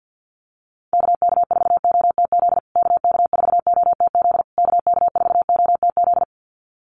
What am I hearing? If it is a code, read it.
Morse code, heard as a keyed tone, about 50 wpm, 700 Hz. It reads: KK4OTZ KK4OTZ KK4OTZ
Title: USA Call Signs Repeated 3x 50wpm